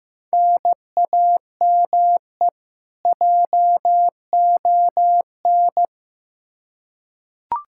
Morse code, heard as a keyed tone, about 15 wpm, 700 Hz. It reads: NAME JON
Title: K1USN SST QSO Practice 15wpm